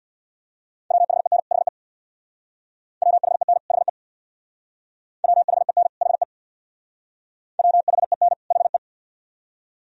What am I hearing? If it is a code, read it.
Morse code, heard as a keyed tone, about 62 wpm, 700 Hz. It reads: WHEN HE WHEN HE WHEN HE WHEN HE
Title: Speed-Racing - Sets of 2 Words 50wpm